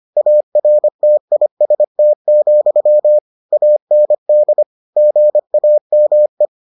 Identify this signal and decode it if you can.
Morse code, heard as a keyed tone, about 25 wpm, 600 Hz. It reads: ARTIST, AND GAME